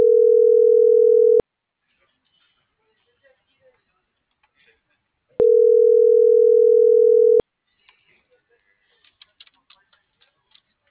Ringback tone (North America), 440 + 480 Hz.